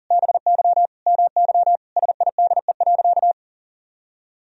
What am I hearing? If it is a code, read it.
Morse code, heard as a keyed tone, about 40 wpm, 700 Hz. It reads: BY MY SIDE.
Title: Sentences from Top 1000 Words - Review Part 1 - 40wpm